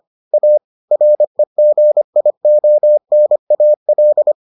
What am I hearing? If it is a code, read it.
Morse code, heard as a keyed tone, about 25 wpm, 600 Hz. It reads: A REGIONAL